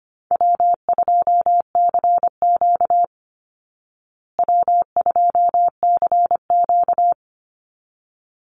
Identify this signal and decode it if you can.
Morse code, heard as a keyed tone, about 25 wpm, 700 Hz. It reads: W2CQ W2CQ
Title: Call Signs Repeated 3x - 25wpm